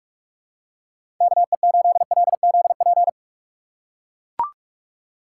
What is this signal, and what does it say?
Morse code, heard as a keyed tone, about 42 wpm, 700 Hz. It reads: KE8LZP
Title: USA Call Signs - 6 Characters - 2x Character Spacing - 45wpm